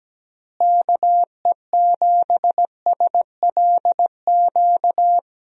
Morse code, heard as a keyed tone, about 17 wpm, 700 Hz. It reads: KE7SLQ
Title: USA Call Signs - 6 Characters - Repeated 3x - 17wpm